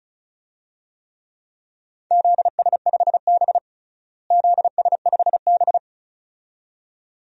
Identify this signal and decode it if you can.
Morse code, heard as a keyed tone, about 34 wpm, 700 Hz. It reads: ZS5B ZS5B
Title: DXCC-100 Call Signs - 4 Characters - Repeated 2x - 35wpm